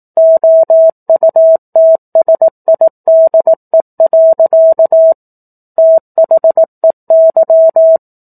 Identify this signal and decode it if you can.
Morse code, heard as a keyed tone, about 18 wpm, 650 Hz. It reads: OUTSIDE. THEY
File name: 04-Chapter-04 18 WPM